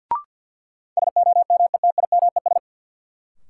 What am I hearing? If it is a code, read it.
Morse code, heard as a keyed tone, about 53 wpm, 700 Hz. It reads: SOMETIMES
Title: Sentences from Top 600 Words - New Word Drill - 50wpm